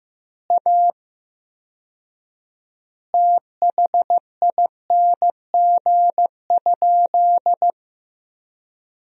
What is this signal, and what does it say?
Morse code, heard as a keyed tone, about 16 wpm, 700 Hz. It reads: A THING?
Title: 4x Word-Spacing - Sentences from Top 200 Words - 15wpm